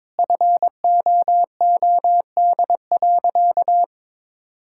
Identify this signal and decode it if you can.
Morse code, heard as a keyed tone, about 22 wpm, 700 Hz. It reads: FOOD.
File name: practice / 22wpm / Sentences with 3-Word Phrases - 22wpm